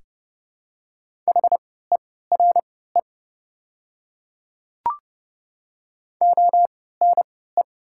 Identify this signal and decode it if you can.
Morse code, heard as a keyed tone, about 30 wpm, spaced as for 17 wpm, 700 Hz. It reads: HERE ONE
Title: 3x Character Spacing - Top 300 Words - 30wpm